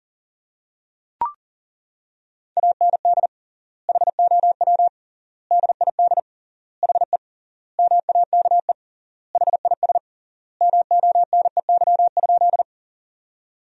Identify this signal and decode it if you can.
Morse code, heard as a keyed tone, about 37 wpm, 700 Hz. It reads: AND HOW DID HE MAKE HIS MONEY?
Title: 3x Word-Spacing - Sentences from Top 400 Words - 40wpm